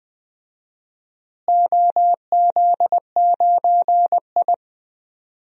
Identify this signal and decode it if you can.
Morse code, heard as a keyed tone, about 20 wpm, 700 Hz. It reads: OZ9I
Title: DXCC-100 Call Signs - 4 Characters - Repeated 3x - 20wpm